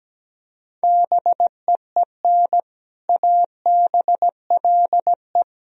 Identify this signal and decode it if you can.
Morse code, heard as a keyed tone, about 17 wpm, 700 Hz. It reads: BEEN ABLE
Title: Speed-Racing - Sets of 3 Words 17wpm